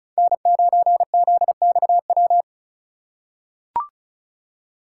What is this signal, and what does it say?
Morse code, heard as a keyed tone, about 36 wpm, 700 Hz. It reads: N9ZXW
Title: USA Call Signs - 5 Characters - 3x Character Spacing - 35wpm